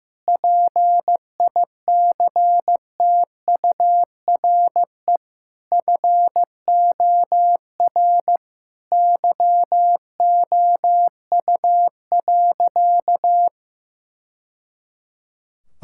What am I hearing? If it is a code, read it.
Morse code, heard as a keyed tone, about 15 wpm, 700 Hz. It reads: PICTURE FOR YOU.
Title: Sentences from Top 1000 Words - Review Part 6 - 15wpm